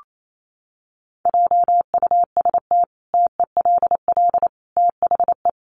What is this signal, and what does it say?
Morse code, heard as a keyed tone, about 28 wpm, 700 Hz. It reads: JUST TELL THE
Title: Sentences from Top 800 Words 28wpm